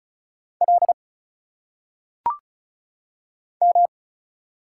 Morse code, heard as a keyed tone, about 37 wpm, 700 Hz. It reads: L M
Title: Single Letters - Phonetic Pronunciation 35wpm